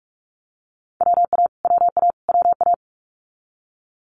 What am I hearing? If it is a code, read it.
Morse code, heard as a keyed tone, about 46 wpm, 700 Hz. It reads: WA WA WA
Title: Multi-letter Phonemes Repeated 3x 45wpm